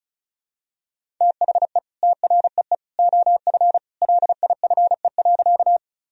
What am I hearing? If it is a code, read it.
Morse code, heard as a keyed tone, about 34 wpm, 700 Hz. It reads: THE TREE OF LIFE.
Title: Sentences from Top 1000 Words - Review Part 1 - 35wpm